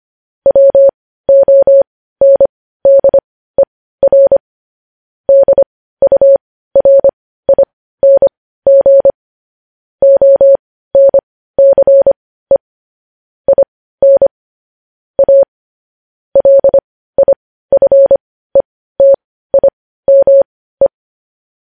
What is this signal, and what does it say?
Morse code, heard as a keyed tone, about 25 wpm, spaced as for 15 wpm, 550 Hz. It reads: WONDER DURING ONCE IN A LIFETIME